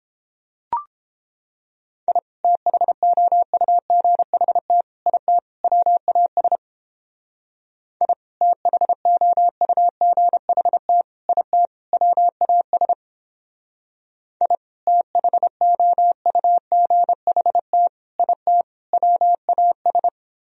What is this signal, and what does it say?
Morse code, heard as a keyed tone, about 31 wpm, 700 Hz. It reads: I THOUGHT IT WAS I THOUGHT IT WAS I THOUGHT IT WAS